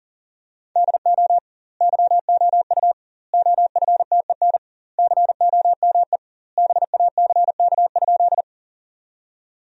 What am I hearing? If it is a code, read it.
Morse code, heard as a keyed tone, about 40 wpm, 700 Hz. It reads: DO YOU OFTEN COME BACK?
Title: 2x Word-Spacing - Sentences from Top 300 Words - 40wpm